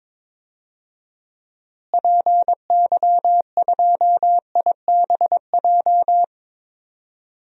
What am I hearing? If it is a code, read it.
Morse code, heard as a keyed tone, about 22 wpm, 700 Hz. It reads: PY2IBJ
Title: DXCC-100 Call Signs - 6 Characters - Repeated 3x - 22wpm